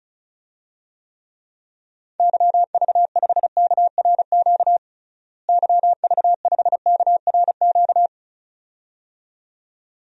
Morse code, heard as a keyed tone, about 36 wpm, 700 Hz. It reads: YV5KRQ YV5KRQ
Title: DXCC-100 Call Signs - 6 Characters - Repeated 2x - 35wpm